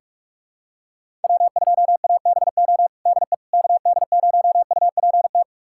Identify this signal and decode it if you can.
Morse code, heard as a keyed tone, about 41 wpm, 700 Hz. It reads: W2ABO DE KD0UPT
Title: USA Callsign DE Callsign 45wpm